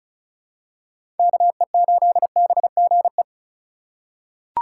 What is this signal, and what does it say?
Morse code, heard as a keyed tone, about 34 wpm, 700 Hz. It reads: KE8BGE
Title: USA Call Signs - 6 Characters - 2x Character Spacing - 35wpm